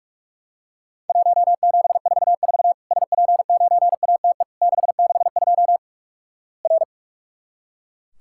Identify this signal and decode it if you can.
Morse code, heard as a keyed tone, about 44 wpm, 700 Hz. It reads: 1744 SP9ATE 662 R
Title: Worked All Europe DX Contest - QTC Exchange - 45wpm